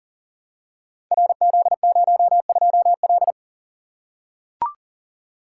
Morse code, heard as a keyed tone, about 39 wpm, 700 Hz. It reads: RZ02L
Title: Code Groups - 5 Letter:Numbers - 40wpm